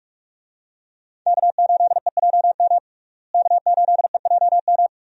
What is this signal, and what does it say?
Morse code, heard as a keyed tone, about 46 wpm, 700 Hz. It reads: K8EJM K8EJM
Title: USA Call Signs - 5 Characters - Repeated 2x - 45wpm